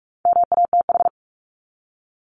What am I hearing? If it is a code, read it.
Morse code, heard as a keyed tone, about 44 wpm, 700 Hz. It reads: MATH
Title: Top 751-1000 Words 45wpm